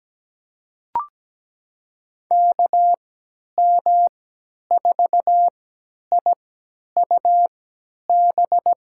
Morse code, heard as a keyed tone, about 17 wpm, spaced as for 10 wpm, 700 Hz. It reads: KM4IUB
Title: USA Call Signs - 6 Characters - 3x Character Spacing - 17wpm